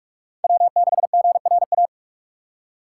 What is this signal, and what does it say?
Morse code, heard as a keyed tone, about 44 wpm, 700 Hz. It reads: W6GRA